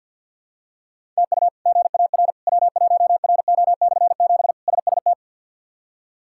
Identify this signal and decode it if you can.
Morse code, heard as a keyed tone, about 46 wpm, 690 Hz. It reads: TU GAR W1RO/7 SST
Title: K1USN SST QSO Practice - 50wpm